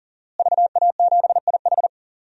Morse code, heard as a keyed tone, about 40 wpm, 700 Hz. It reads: VA7IH
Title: DXCC-100 Call Signs - 5 Characters - Repeated 2x - 40wpm